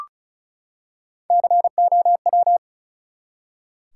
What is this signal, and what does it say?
Morse code, heard as a keyed tone, about 36 wpm, 700 Hz. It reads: COW
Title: Three Letter Words - 35wpm